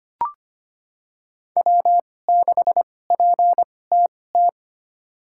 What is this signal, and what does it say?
Morse code, heard as a keyed tone, about 25 wpm, spaced as for 18 wpm, 700 Hz. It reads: W6PTT
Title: USA Call Signs - 5 Characters - 2x Character Spacing - 25wpm